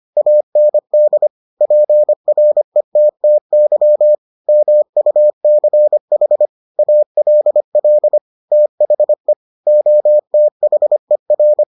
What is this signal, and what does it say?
Morse code, heard as a keyed tone, about 25 wpm, 600 Hz. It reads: AND PRETTY MUCH ALL THE OTHER